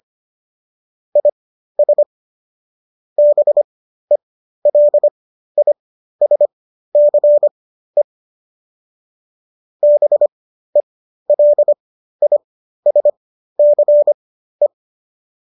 Morse code, heard as a keyed tone, about 25 wpm, spaced as for 13 wpm, 590 Hz. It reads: IS BELISCE BELISCE
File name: qso219 13